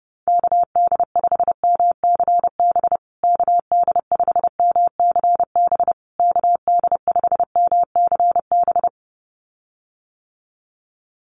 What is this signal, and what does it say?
Morse code, heard as a keyed tone, about 30 wpm, 700 Hz. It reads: KD5MCB KD5MCB KD5MCB
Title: USA Call Signs Repeated 3x 30wpm